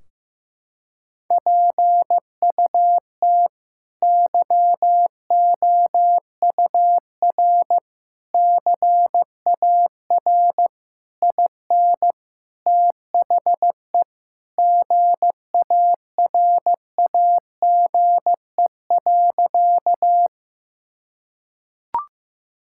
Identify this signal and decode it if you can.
Morse code, heard as a keyed tone, about 15 wpm, 700 Hz. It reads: PUT YOUR CAR IN THE GARAGE.
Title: Sentences from Top 1600 Words 15wpm